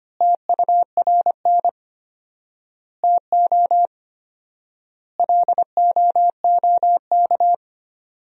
Morse code, heard as a keyed tone, about 25 wpm, 700 Hz. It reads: TURN TO LOOK
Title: 4x Word-Spacing - Sentences from Top 200 Words - 25wpm